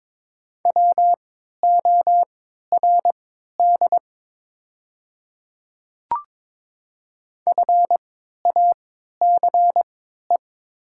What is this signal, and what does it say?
Morse code, heard as a keyed tone, about 22 wpm, spaced as for 13 wpm, 700 Hz. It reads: WORD FACE
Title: 3x Character Spacing - Top 300 Words - 22wpm